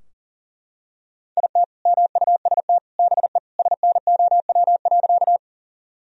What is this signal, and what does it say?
Morse code, heard as a keyed tone, about 40 wpm, 700 Hz. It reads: IT MUST BE SNOW.
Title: Sentences from Top 1000 Words - Review Part 3 - 40wpm